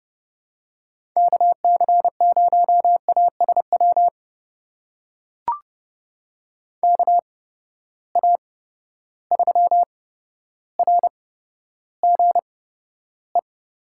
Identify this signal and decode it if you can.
Morse code, heard as a keyed tone, about 30 wpm, spaced as for 13 wpm, 700 Hz. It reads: KC0ASW KA3RGE
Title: USA Call Signs - 8x Character Spacing - 30wpm